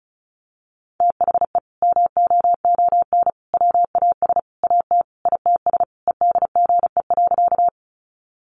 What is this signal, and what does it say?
Morse code, heard as a keyed tone, about 34 wpm, 700 Hz. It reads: THE MOON WAS AT ITS EDGE.
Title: Sentences from Top 1300 Words 35wpm